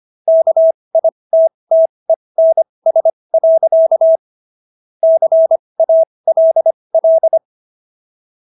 Morse code, heard as a keyed tone, about 25 wpm, spaced as for 20 wpm, 650 Hz. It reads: KITTENS. CALL